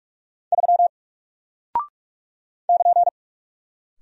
Morse code, heard as a keyed tone, about 48 wpm, 700 Hz. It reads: 3 <KN>
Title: Single Character (Letter:Number:Punctuation:Prosign) Alt 45wpm